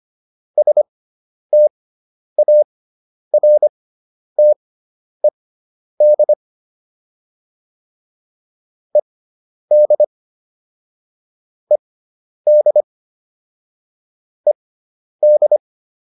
Morse code, heard as a keyed tone, about 25 wpm, spaced as for 10 wpm, 600 Hz. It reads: STARTED ED ED ED